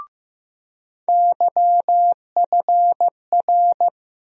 Morse code, heard as a keyed tone, about 15 wpm, 700 Hz. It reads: YFR
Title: USA Call Sign Postfixes - 3 Letters 15wpm